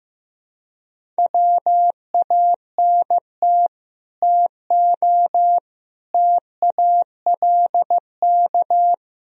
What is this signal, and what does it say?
Morse code, heard as a keyed tone, about 15 wpm, 700 Hz. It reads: WANT TO TALK
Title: Speed-Racing - Sets of 4 Words 15wpm